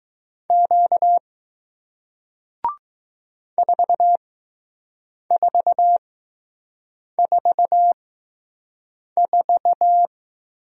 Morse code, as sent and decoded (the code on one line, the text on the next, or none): --.- / ....- ....- ....- ....-
Q 4444